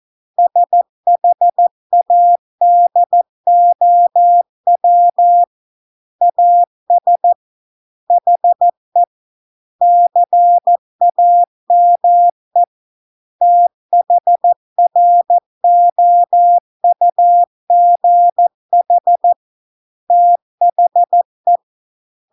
Morse code, heard as a keyed tone, about 14 wpm, 700 Hz. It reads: SHADOW AS HE CAME THROUGH THE